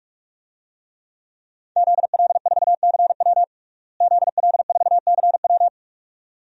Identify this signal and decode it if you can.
Morse code, heard as a keyed tone, about 46 wpm, 700 Hz. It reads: ZL4CW ZL4CW
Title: DXCC-100 Call Signs - 5 Characters - Repeated 2x - 45wpm